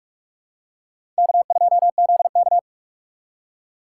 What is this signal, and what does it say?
Morse code, heard as a keyed tone, about 44 wpm, 690 Hz. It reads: K2ZK
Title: USA Call Signs - 4 Characters - 3x Character Spacing - 45wpm